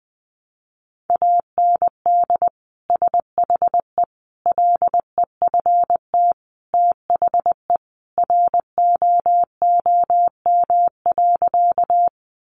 Morse code, heard as a keyed tone, about 20 wpm, 700 Hz. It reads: AND SHE LEFT THE ROOM.
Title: Sentences from Top 300 Words 20wpm